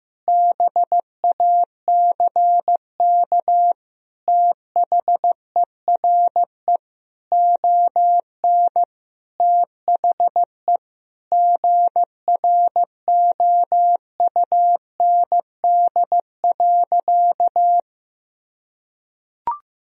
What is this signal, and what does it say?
Morse code, heard as a keyed tone, about 15 wpm, 700 Hz. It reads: BACK THERE ON THE GROUND.
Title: Sentences from Top 1000 Words - Review Part 2 - 15wpm